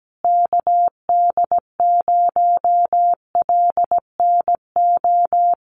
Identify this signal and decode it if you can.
Morse code, heard as a keyed tone, about 17 wpm, 700 Hz. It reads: KD0LNO
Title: USA Call Signs Repeated 3x 17wpm